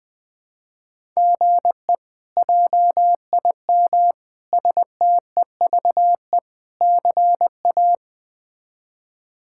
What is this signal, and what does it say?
Morse code, heard as a keyed tone, about 20 wpm, 700 Hz. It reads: GE JIM STEVE CA
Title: K1USN SST QSO Practice - 20wpm